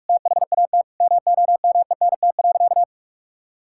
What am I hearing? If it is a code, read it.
Morse code, heard as a keyed tone, about 41 wpm, 700 Hz. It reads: THAT MOMENT.
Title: Sentences from Top 1000 Words - Review Part 6 - 45wpm